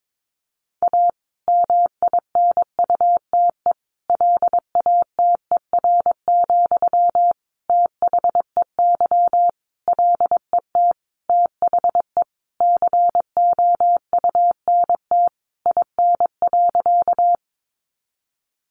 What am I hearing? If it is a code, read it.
Morse code, heard as a keyed tone, about 22 wpm, 700 Hz. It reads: A MINUTE LATER, THEY LET THE COUNT IN.
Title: Sentences from Top 1500 Words 22wpm